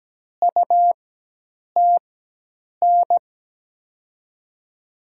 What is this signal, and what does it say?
Morse code, heard as a keyed tone, about 17 wpm, spaced as for 8 wpm, 700 Hz. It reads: UTN
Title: USA Call Signs - 6 Characters - 4x Character Spacing - 17wpm